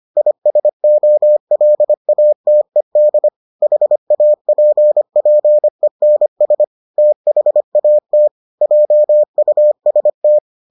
Morse code, heard as a keyed tone, about 25 wpm, 600 Hz. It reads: ISOLATED HAPPENS THAT JUST